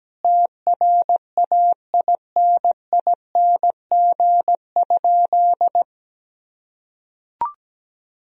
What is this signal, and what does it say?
Morse code, heard as a keyed tone, about 17 wpm, 700 Hz. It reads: TRAINING?
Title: Sentences from Top 1000 Words - Review Part 4 - 17wpm